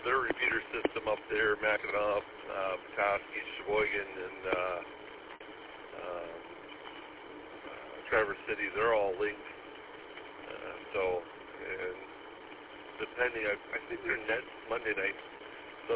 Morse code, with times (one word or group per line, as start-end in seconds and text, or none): none